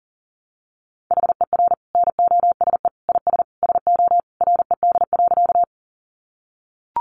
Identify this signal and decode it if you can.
Morse code, heard as a keyed tone, about 40 wpm, 700 Hz. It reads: HER NOSE IS SO RED.
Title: Easy Phrases 40wpm